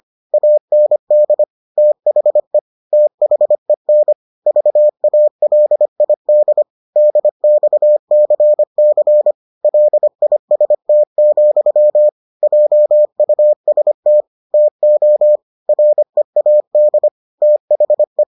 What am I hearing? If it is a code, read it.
Morse code, heard as a keyed tone, about 25 wpm, 600 Hz. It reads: AND THE THEN VALID DXCC LIST, JUST TO READ THE